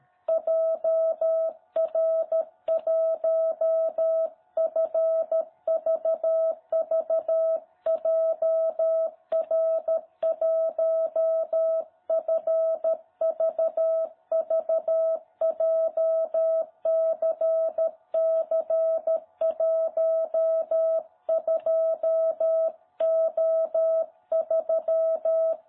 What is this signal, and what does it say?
Morse code, heard as a keyed tone, about 13 wpm, 660 Hz. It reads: JR1FVV JR1FVV JCC12O3
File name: jh8cbh / 2021cw